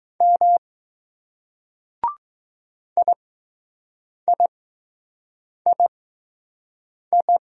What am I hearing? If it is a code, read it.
Morse code, heard as a keyed tone, about 20 wpm, spaced as for 6 wpm, 700 Hz. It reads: M IIII